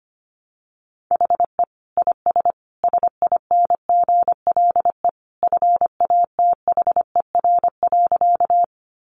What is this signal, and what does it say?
Morse code, heard as a keyed tone, about 25 wpm, 700 Hz. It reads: HE IS SINGLE FATHER.
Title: Sentences from Top 700 Words 25wpm